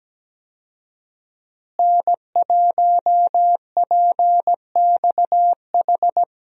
Morse code, heard as a keyed tone, about 17 wpm, 700 Hz. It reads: N1PXH
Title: USA Call Signs - 5 Characters - Repeated 3x - 17wpm